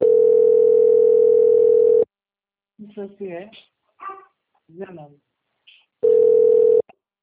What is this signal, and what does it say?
Ringback tone (North America), 440 + 480 Hz, with speech.
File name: out- - - - - .6073